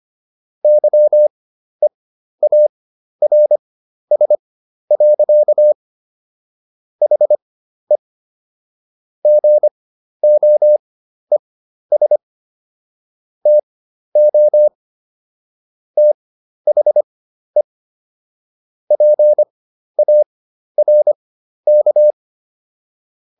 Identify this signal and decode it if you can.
Morse code, heard as a keyed tone, about 25 wpm, spaced as for 12 wpm, 600 Hz. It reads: YEARS. HE GOES TO THE PARK